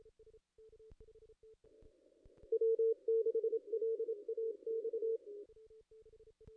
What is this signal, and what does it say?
Morse code, heard as a keyed tone, about 28 wpm, 440 Hz. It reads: W6LAX
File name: contests / ss-cw-k5tr-op-n5rz -2200